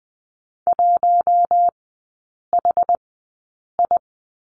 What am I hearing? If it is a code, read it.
Morse code, heard as a keyed tone, about 20 wpm, spaced as for 8 wpm, 700 Hz. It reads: 1HI
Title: 074 Lesson 13e - Letter L - Three Characters 20wpm